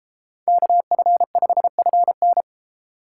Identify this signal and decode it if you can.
Morse code, heard as a keyed tone, about 31 wpm, 700 Hz. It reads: KF5FN